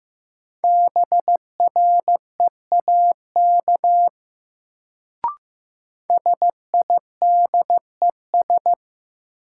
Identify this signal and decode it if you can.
Morse code, heard as a keyed tone, about 15 wpm, 700 Hz. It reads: BREAK SIDES
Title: Sets of 4 Words - New Word Drill - 15wpm